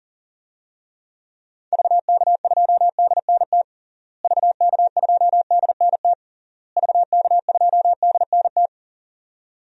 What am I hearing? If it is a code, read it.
Morse code, heard as a keyed tone, about 40 wpm, 700 Hz. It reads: VK2DNT VK2DNT VK2DNT
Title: DXCC-100 Call Signs - 6 Characters - Repeated 3x - 40wpm